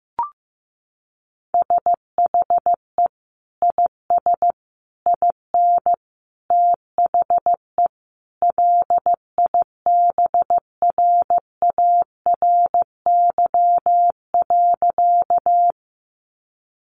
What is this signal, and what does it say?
Morse code, heard as a keyed tone, about 15 wpm, 700 Hz. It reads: SHE IS IN THE LIBRARY.
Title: Sentences from Top 600 Words 15wpm